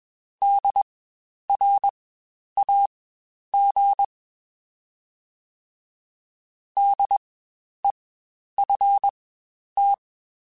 Morse code, heard as a keyed tone, about 22 wpm, spaced as for 10 wpm, 800 Hz. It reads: DRAG DEFT
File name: Spaced Alphas / Spaced Alphas mp3s / Ds sp @20